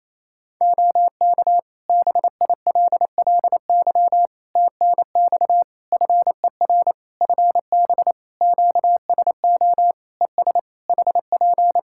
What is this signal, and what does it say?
Morse code, heard as a keyed tone, about 28 wpm, 700 Hz. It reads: OK BILLY TNX FER FB QSO ES HP
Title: Practice QSO - Exchange 4 - Ending - 28wpm